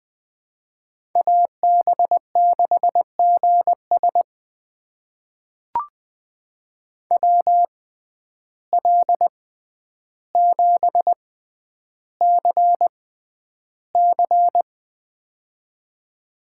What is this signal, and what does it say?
Morse code, heard as a keyed tone, about 20 wpm, spaced as for 10 wpm, 700 Hz. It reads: AB6GS WL7CC
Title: USA Call Signs - 6x Character Spacing - 20wpm